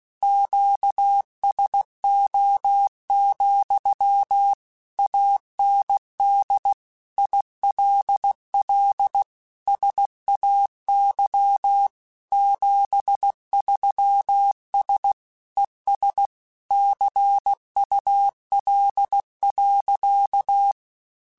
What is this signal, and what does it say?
Morse code, heard as a keyed tone, about 16 wpm, 800 Hz. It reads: QSO, AND ILL SAY 73S ES CUL.
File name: QSOs / QSOs-mp3s / WA7QOX 2@15